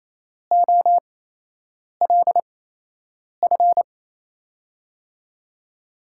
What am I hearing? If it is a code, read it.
Morse code, heard as a keyed tone, about 28 wpm, spaced as for 8 wpm, 700 Hz. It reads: OLF
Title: USA Call Signs - 5 Characters - 8x Character Spacing - 28wpm